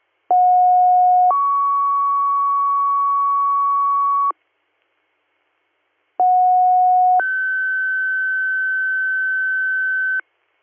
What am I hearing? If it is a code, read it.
Two-tone page, heard as a one-second tone, then a three-second tone, about 730 then 1120 Hz, about 730 then 1550 Hz. It is heard from 0.3 to 10.2 s.